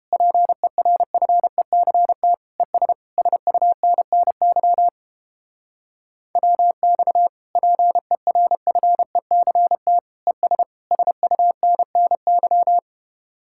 Morse code, heard as a keyed tone, about 32 wpm, 700 Hz. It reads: PERFECT ES SUNNY WX PERFECT ES SUNNY